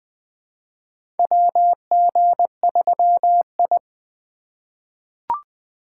Morse code, heard as a keyed tone, about 20 wpm, 700 Hz. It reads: WG3I
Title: USA Call Signs - 4 Characters - 3x Character Spacing - 20wpm